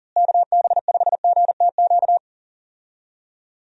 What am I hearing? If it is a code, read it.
Morse code, heard as a keyed tone, about 40 wpm, 700 Hz. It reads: KB5GTQ
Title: USA Call Signs - 6 Characters - Repeated 2x - 40wpm